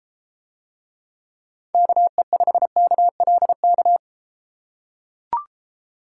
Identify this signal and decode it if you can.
Morse code, heard as a keyed tone, about 33 wpm, 700 Hz. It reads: KE5KLK